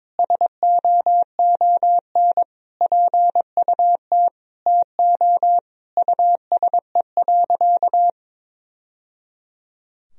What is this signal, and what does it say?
Morse code, heard as a keyed tone, about 22 wpm, 700 Hz. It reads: SOON PUT TO USE.
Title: Sentences from Top 1000 Words - Review Part 3 - 22wpm